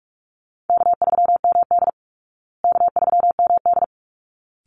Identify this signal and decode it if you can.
Morse code, heard as a keyed tone, about 44 wpm, 700 Hz. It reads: K3MD K3MD
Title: Call Signs Repeated 3x - 45wpm